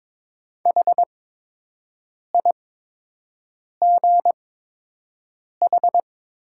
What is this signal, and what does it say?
Morse code, heard as a keyed tone, about 22 wpm, spaced as for 6 wpm, 700 Hz. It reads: HIGH
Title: 8x Character Spacing - Top 200 Words - 22wpm